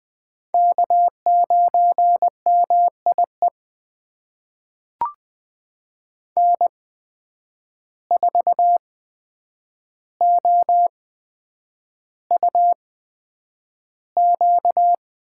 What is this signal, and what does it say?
Morse code, heard as a keyed tone, about 20 wpm, spaced as for 9 wpm, 700 Hz. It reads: K9MIE N4OUQ
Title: USA Call Signs - 5 Characters - 8x Character Spacing - 20wpm